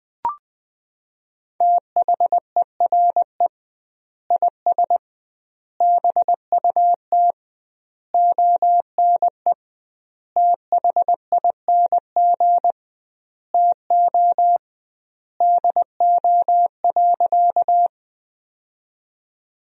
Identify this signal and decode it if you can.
Morse code, heard as a keyed tone, about 20 wpm, 700 Hz. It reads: THERE IS BUT ONE THING TO DO.
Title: 2x Word-Spacing - Sentences from Top 100 Words - 20wpm